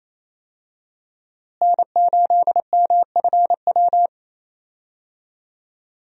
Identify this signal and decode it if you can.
Morse code, heard as a keyed tone, about 28 wpm, 700 Hz. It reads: N8MFW
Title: USA Call Signs - 5 Characters - Repeated 3x - 28wpm